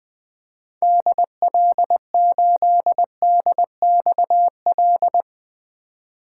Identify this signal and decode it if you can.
Morse code, heard as a keyed tone, about 21 wpm, 700 Hz. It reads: DL8DXL
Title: 2023 SOTA Chaser Call Signs - 20wpm